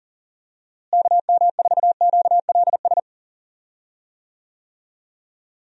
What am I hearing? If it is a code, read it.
Morse code, heard as a keyed tone, about 37 wpm, 700 Hz. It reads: KM4QLS
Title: USA Call Signs - 6 Characters - Repeated 2x - 40wpm